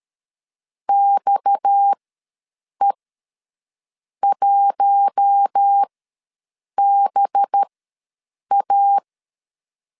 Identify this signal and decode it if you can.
Morse code, heard as a keyed tone, about 13 wpm, spaced as for 6 wpm, 800 Hz. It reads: XE1BA
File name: LEZIONI / 8 YJ45 / 8 Y ws 12wpm 3409